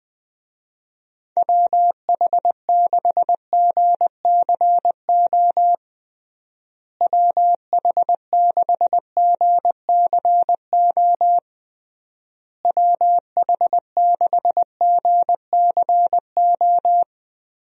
Morse code, heard as a keyed tone, about 21 wpm, 700 Hz. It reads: WH6GCO WH6GCO WH6GCO
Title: USA Call Signs - 6 Characters - Repeated 3x - 20wpm